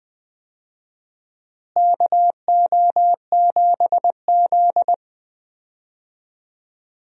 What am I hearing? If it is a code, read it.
Morse code, heard as a keyed tone, about 20 wpm, 700 Hz. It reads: KO7Z
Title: USA Call Signs - 4 Characters - Repeated 2x - 20wpm